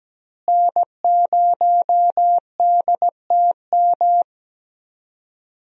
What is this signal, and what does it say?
Morse code, heard as a keyed tone, about 17 wpm, 700 Hz. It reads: N0DTM
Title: USA Call Signs - 5 Characters - Repeated 3x - 17wpm